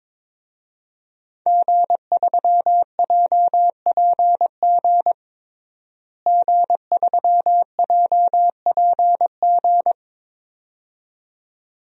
Morse code, heard as a keyed tone, about 21 wpm, 700 Hz. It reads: G3JPG G3JPG
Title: DXCC-100 Call Signs - 5 Characters - Repeated 2x - 22wpm